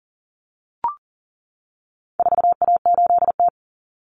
Morse code, heard as a keyed tone, about 40 wpm, 700 Hz. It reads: 4A8T
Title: Common Cut Numbers - 4-Digit 40wpm